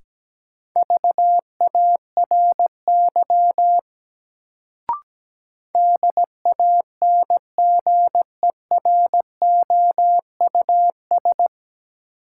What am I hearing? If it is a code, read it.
Morse code, heard as a keyed tone, about 17 wpm, 700 Hz. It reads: VARY DANGEROUS
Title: Sentences from Top 1200 Words - New Word Drill Alt - 17wpm